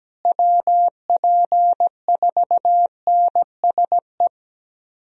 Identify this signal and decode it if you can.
Morse code, heard as a keyed tone, about 17 wpm, 700 Hz. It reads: WP4NSE
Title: USA Call Signs - 6 Characters - Repeated 2x - 17wpm